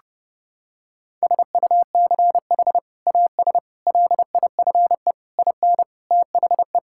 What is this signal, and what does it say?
Morse code, heard as a keyed tone, about 30 wpm, 700 Hz. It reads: SUCH AS LIFE IN THE
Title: Sentences from Top 1000 Words - Review Part 5 - 30wpm